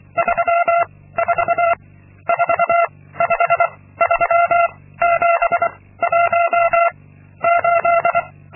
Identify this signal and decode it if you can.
Morse code, heard as a keyed tone, about 24 wpm, 680 Hz. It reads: 3 4 4 5 3 7 1 8
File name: Linksmakalnis-2